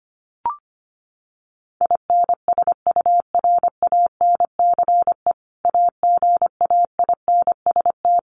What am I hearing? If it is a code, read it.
Morse code, heard as a keyed tone, about 25 wpm, 700 Hz. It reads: INSURANCE AGAINST
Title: Sentences from Top 1100 Words 25wpm